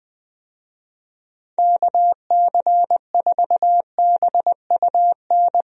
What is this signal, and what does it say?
Morse code, heard as a keyed tone, about 20 wpm, 700 Hz. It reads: KC4BUN
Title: USA Call Signs - 6 Characters - Repeated 2x - 20wpm